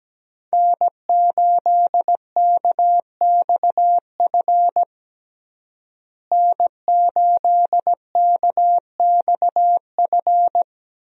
Morse code, heard as a keyed tone, about 17 wpm, 700 Hz. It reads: N8KXF N8KXF
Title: USA Call Signs - 5 Characters - Repeated 3x - 17wpm